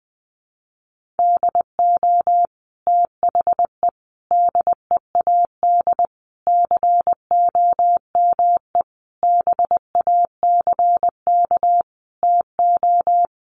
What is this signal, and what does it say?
Morse code, heard as a keyed tone, about 20 wpm, 700 Hz. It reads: DO THE DEAD COME BACK TO
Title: Sentences from Top 1500 Words 20wpm